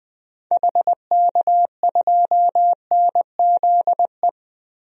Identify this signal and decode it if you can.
Morse code, heard as a keyed tone, about 21 wpm, 700 Hz. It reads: HK2NZE
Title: DXCC-100 Call Signs - 6 Characters - 20wpm